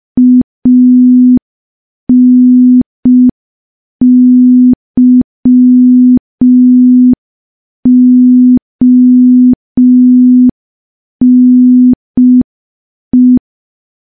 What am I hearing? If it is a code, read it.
Morse code, heard as a keyed tone, about 5 wpm, 250 Hz. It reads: ANYONE
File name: morse / morse-training-05WPM